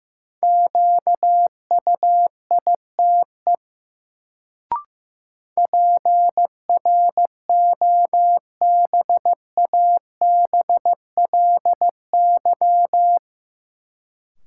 Morse code, heard as a keyed tone, about 15 wpm, 700 Hz. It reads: QUITE PROBABLY
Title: Sentences from Top 700 Words - New Word Drill - 15wpm